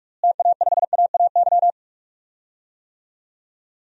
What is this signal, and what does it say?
Morse code, heard as a keyed tone, about 46 wpm, 700 Hz. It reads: TA5AAY
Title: DXCC-100 Call Signs - 6 Characters - Repeated 2x - 45wpm